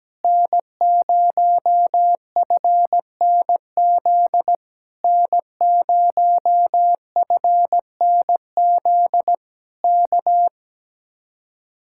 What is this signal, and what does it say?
Morse code, heard as a keyed tone, about 17 wpm, 700 Hz. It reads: N0FNZ N0FNZ K
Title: Speed-Racing Common QSO Phrases - 17wpm